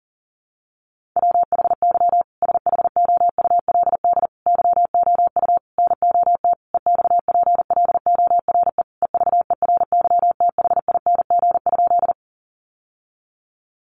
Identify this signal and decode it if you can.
Morse code, heard as a keyed tone, about 40 wpm, 700 Hz. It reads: WHY SHOULD YOU NOT EXPLORE EVERYTHING?
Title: Sentences from Top 1100 Words 40wpm